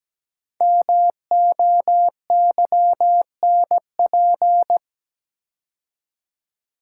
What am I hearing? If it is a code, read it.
Morse code, heard as a keyed tone, about 17 wpm, 700 Hz. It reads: MOYNP
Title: Code Groups - 5 Letter:Numbers - 17wpm